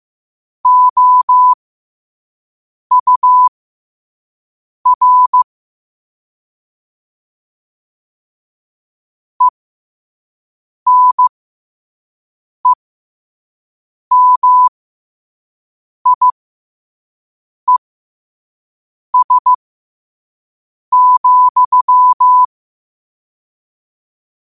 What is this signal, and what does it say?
Morse code, heard as a keyed tone, about 15 wpm, spaced as for 5 wpm, 1000 Hz. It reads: OUR ENEMIES,
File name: qotdfiles / 5wpm